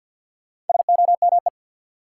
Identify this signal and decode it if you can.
Morse code, heard as a keyed tone, about 53 wpm, 700 Hz. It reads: SOME
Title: Sentences from Top 100 Words - New Word Drill - 50wpm